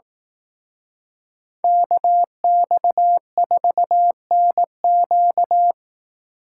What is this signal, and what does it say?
Morse code, heard as a keyed tone, about 18 wpm, 700 Hz. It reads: KX4NQ
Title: Speed Racing - USA 2x2 Call Signs 15wpm